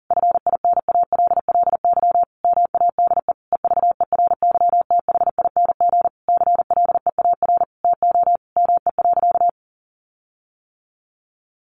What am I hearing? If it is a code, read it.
Morse code, heard as a keyed tone, about 40 wpm, 700 Hz. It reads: FINALLY MADE EVERYTHING CLEAR TO ME.
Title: Sentences from Top 1300 Words 40wpm